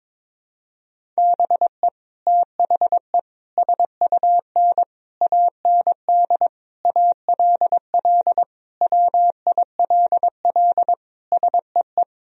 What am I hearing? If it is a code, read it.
Morse code, heard as a keyed tone, about 22 wpm, 700 Hz. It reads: BE THE SUN AND ALL WILL SEE
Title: Sentences from Top 1000 Words - Review Part 1 - 22wpm